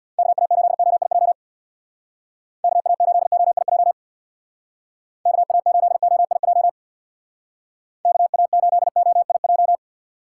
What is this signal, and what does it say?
Morse code, heard as a keyed tone, about 64 wpm, 700 Hz. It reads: KA8OIJ KA8OIJ KA8OIJ KA8OIJ